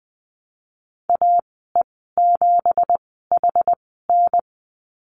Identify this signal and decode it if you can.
Morse code, heard as a keyed tone, about 19 wpm, spaced as for 14 wpm, 700 Hz. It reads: AE7HN
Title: 193 Lesson 31g - Letter v - Call Signs 20wpm